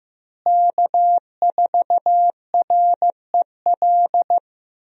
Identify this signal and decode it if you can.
Morse code, heard as a keyed tone, about 15 wpm, 700 Hz. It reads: K4REL
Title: USA Call Signs - 5 Characters - Repeated 3x - 15wpm